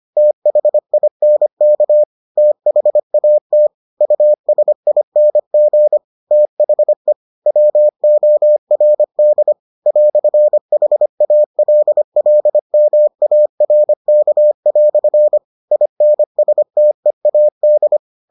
Morse code, heard as a keyed tone, about 25 wpm, 600 Hz. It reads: THINK THAT USING THE WORD "HALLMARK" INSTEAD